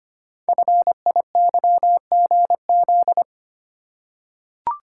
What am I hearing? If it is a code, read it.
Morse code, heard as a keyed tone, about 25 wpm, 700 Hz. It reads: FIYGZ
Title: Code Groups - 5 Letters - 25wpm